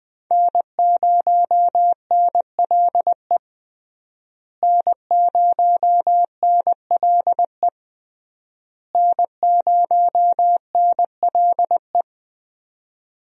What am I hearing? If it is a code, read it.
Morse code, heard as a keyed tone, about 20 wpm, 700 Hz. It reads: N0NLE N0NLE N0NLE
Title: USA Call Signs - 5 Characters - Repeated 3x - 20wpm